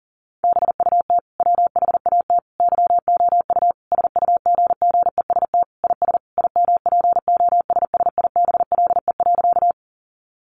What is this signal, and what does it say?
Morse code, heard as a keyed tone, about 40 wpm, 700 Hz. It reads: BUT WHAT YOU SUGGEST IS IMPOSSIBLE.
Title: Sentences from Top 1300 Words 40wpm